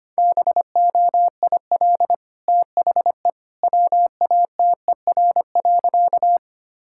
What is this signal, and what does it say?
Morse code, heard as a keyed tone, about 25 wpm, 690 Hz. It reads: BOIL THE WATER.
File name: Easy Sentences with Hints - 25wpm